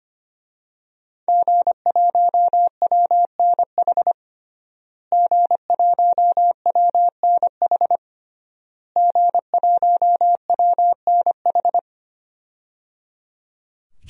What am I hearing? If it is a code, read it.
Morse code, heard as a keyed tone, about 25 wpm, 700 Hz. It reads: G1WNH G1WNH G1WNH
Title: DXCC-100 Call Signs - 5 Characters - Repeated 3x - 25wpm